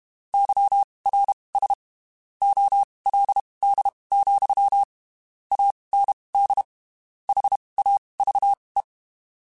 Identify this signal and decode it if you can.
Morse code, heard as a keyed tone, about 32 wpm, spaced as for 23 wpm, 800 Hz. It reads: YRS OLD, AND HAVE
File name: (spaced) / QSOs Spaced-mp3s / WA7QOX 2 @30 sp